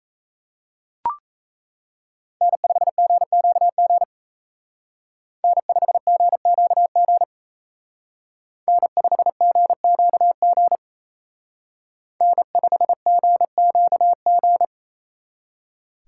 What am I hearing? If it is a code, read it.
Morse code, heard as a keyed tone, about 34 wpm, 700 Hz. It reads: N5GQG N5GQG N5GQG N5GQG
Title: Speed Racing - USA 1x3 Call Signs 28wpm